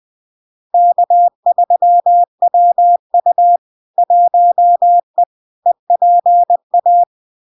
Morse code, heard as a keyed tone, about 20 wpm, 700 Hz. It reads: K3WU 1E EPA
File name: LARK-FD-2022-Log-in-CW-20wpm-700-Hz